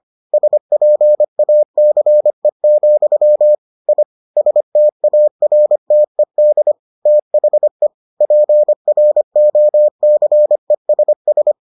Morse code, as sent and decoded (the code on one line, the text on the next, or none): ... .--. .- -.-. --..-- / .. / ... - .- .-. - . -.. / - .... / .--. .-. --- -.-. ... ...
SPACE, I STARTED THE PROCESS